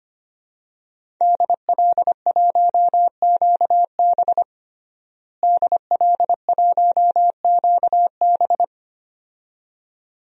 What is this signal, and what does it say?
Morse code, heard as a keyed tone, about 25 wpm, 700 Hz. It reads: DL1QB DL1QB
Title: DXCC-100 Call Signs - 5 Characters - Repeated 2x - 25wpm